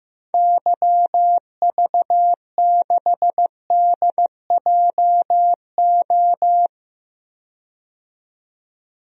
Morse code, heard as a keyed tone, about 15 wpm, 700 Hz. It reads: YV6DJO
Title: DXCC-100 Call Signs - 6 Characters - 15wpm